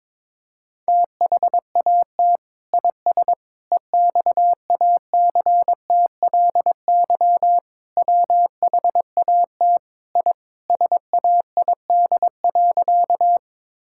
Morse code, heard as a keyed tone, about 21 wpm, 700 Hz. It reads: THAT IS EXACTLY WHAT I SAID.
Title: Sentences from Top 1000 Words - Review Part 6 - 22wpm